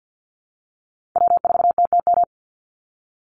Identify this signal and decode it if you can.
Morse code, heard as a keyed tone, about 48 wpm, 700 Hz. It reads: W4TTM
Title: Call Signs Repeated 3x - 50wpm